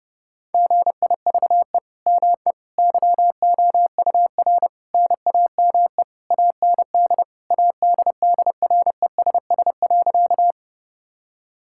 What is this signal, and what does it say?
Morse code, heard as a keyed tone, about 32 wpm, 700 Hz. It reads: GIVE ME YOUR NAME AND ADDRESS.
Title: Sentences from Top 1000 Words - Review Part 6 - 30wpm